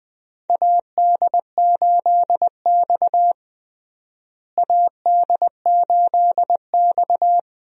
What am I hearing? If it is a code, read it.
Morse code, heard as a keyed tone, about 20 wpm, 700 Hz. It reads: AD8X AD8X
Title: USA Call Signs - 4 Characters - Repeated 3x - 20wpm